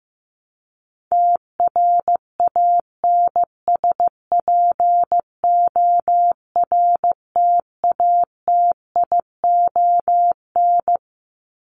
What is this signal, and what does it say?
Morse code, heard as a keyed tone, about 15 wpm, 700 Hz. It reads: TRANSPORTATION
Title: Top 1501-1750 Words 15wpm